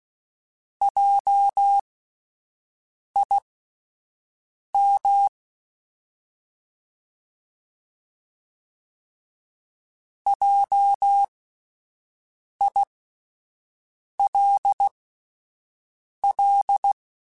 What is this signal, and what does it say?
Morse code, heard as a keyed tone, about 16 wpm, spaced as for 5 wpm, 800 Hz. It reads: JIM JILL